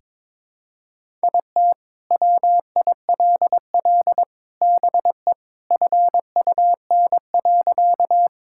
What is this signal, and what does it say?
Morse code, heard as a keyed tone, about 22 wpm, 700 Hz. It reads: IT WILL BE FUN.
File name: practice / 22wpm / Sentences with 3-Word Phrases - 22wpm